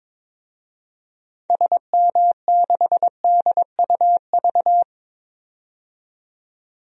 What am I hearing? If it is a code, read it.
Morse code, heard as a keyed tone, about 22 wpm, 700 Hz. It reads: SM6DUV
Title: DXCC-100 Call Signs - 6 Characters - 22wpm